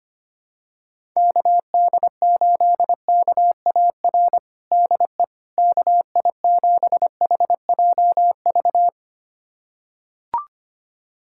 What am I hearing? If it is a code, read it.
Morse code, heard as a keyed tone, about 25 wpm, 700 Hz. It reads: KD8KAR DE KI7HJV
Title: USA Callsign DE Callsign - 6 Characters - 25wpm